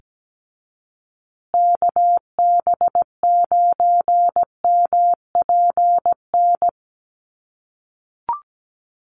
Morse code, heard as a keyed tone, about 17 wpm, 700 Hz. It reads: KB9MPN
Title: USA Call Signs Repeated 3x 17wpm